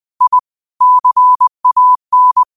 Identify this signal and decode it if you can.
Morse code, heard as a keyed tone, about 20 wpm, 1000 Hz. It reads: I CAN
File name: qotdfiles / 20wpm